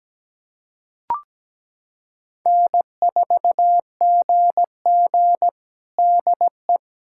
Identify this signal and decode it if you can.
Morse code, heard as a keyed tone, about 17 wpm, 700 Hz. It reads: N4GG DE
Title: USA Callsign DE Callsign - 4 Characters - 17wpm